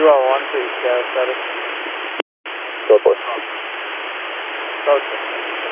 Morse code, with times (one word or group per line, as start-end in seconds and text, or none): none